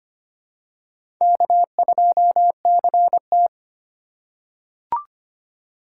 Morse code, heard as a keyed tone, about 25 wpm, 700 Hz. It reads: K2CT
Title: USA Call Signs - 4 Characters - 4x Character Spacing - 25wpm